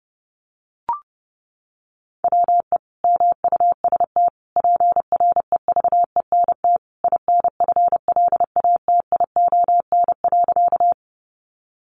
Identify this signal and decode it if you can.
Morse code, heard as a keyed tone, about 30 wpm, 700 Hz. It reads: WE MUST PREVENT INFLATION.
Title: Sentences from Top 1100 Words 30wpm